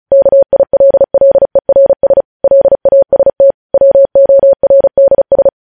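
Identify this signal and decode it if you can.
Morse code, heard as a keyed tone, about 36 wpm, 550 Hz. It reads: KILLERS LAST WORDS
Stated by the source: Morse news headlines practice at 35 words per minute.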